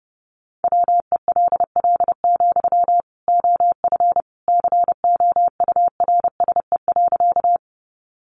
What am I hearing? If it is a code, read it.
Morse code, heard as a keyed tone, about 30 wpm, 700 Hz. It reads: WELL, OF COURSE.
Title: Sentences from Top 500 Words 30wpm